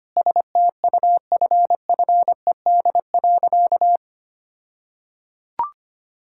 Morse code, heard as a keyed tone, about 25 wpm, 700 Hz. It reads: STUFFED.
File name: practice / 25wpm / All-Age Appropriate Jokes - 25wpm